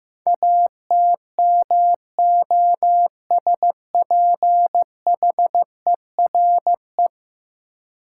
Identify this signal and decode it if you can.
Morse code, heard as a keyed tone, about 15 wpm, 700 Hz. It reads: ATMOSPHERE
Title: Sentences from Top 1600 Words - New Word Drill - 15wpm